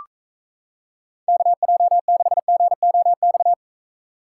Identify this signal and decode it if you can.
Morse code, heard as a keyed tone, about 34 wpm, 700 Hz. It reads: KJ6GOX
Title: Speed Racing - USA 2x3 Call Signs 28wpm